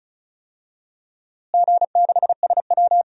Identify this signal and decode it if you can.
Morse code, heard as a keyed tone, about 36 wpm, 700 Hz. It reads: G6SW
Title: DXCC-100 Call Signs - 4 Characters - Repeated 2x - 35wpm